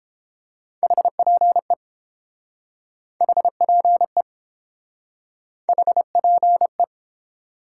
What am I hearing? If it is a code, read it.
Morse code, heard as a keyed tone, about 30 wpm, 700 Hz. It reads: HPE HPE HPE